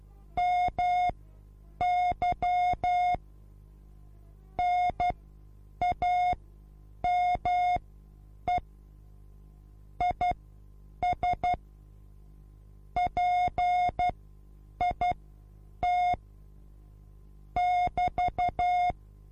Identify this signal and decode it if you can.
Morse code, heard as a keyed tone, about 12 wpm, spaced as for 8 wpm, 720 Hz. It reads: MY NAME IS PIT <BT>